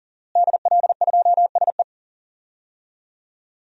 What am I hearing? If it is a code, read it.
Morse code, heard as a keyed tone, about 38 wpm, 700 Hz. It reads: DL2SE
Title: DXCC-100 Call Signs - 5 Characters - Repeated 2x - 40wpm